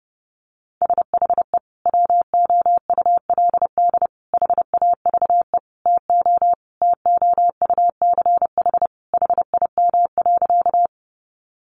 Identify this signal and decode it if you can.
Morse code, heard as a keyed tone, about 30 wpm, 700 Hz. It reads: SHE WOULD HAVE TO TOUCH HIM.
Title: Sentences from Top 1200 Words 30wpm